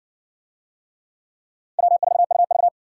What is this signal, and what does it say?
Morse code, heard as a keyed tone, about 56 wpm, 700 Hz. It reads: W4UV